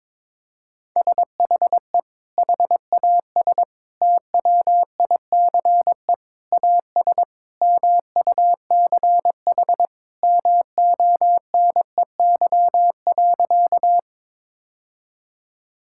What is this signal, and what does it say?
Morse code, heard as a keyed tone, about 22 wpm, 700 Hz. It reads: SHE HAS TWICE AS MUCH MONEY.
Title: Sentences from Top 1600 Words 22wpm